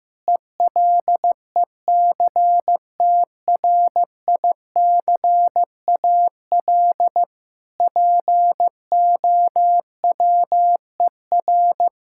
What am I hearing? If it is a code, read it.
Morse code, heard as a keyed tone, about 16 wpm, 700 Hz. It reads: ELECTRICAL POWER
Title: Sentences from Top 1000 Words - Review Part 6 - 15wpm